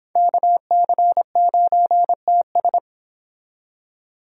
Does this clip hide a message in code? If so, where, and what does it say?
Yes: Morse code from 0.2 to 2.8 s, heard as a keyed tone, about 25 wpm, 700 Hz. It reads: KC9TS